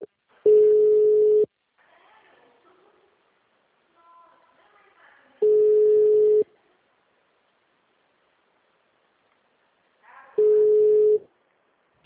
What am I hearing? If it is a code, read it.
Ringback tone (Europe, 425 Hz), with speech.